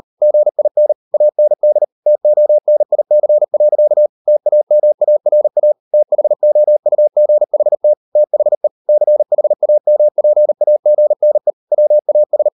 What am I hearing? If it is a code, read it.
Morse code, heard as a keyed tone, about 40 wpm, 600 Hz. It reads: GIN AND TONIC. TAMARA THOUGHT THE CHAMPAGNE WAS